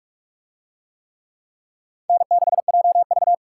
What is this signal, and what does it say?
Morse code, heard as a keyed tone, about 44 wpm, 700 Hz. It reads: N6JV